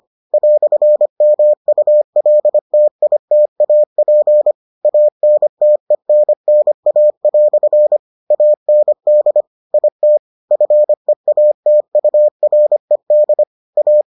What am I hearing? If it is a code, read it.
Morse code, heard as a keyed tone, about 25 wpm, 600 Hz. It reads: "MULTITAP ANTENNA" AND IT FEATURED A